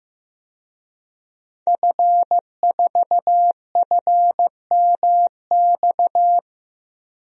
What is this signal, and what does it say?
Morse code, heard as a keyed tone, about 15 wpm, 700 Hz. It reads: F4FMX
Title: DXCC-100 Call Signs - 5 Characters - 15wpm